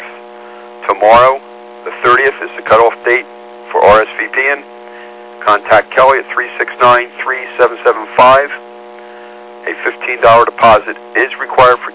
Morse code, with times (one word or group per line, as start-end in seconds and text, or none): none